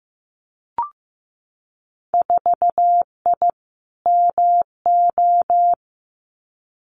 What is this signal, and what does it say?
Morse code, heard as a keyed tone, about 15 wpm, 700 Hz. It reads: 4I MO
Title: Winter Field Day Exchanges 15wpm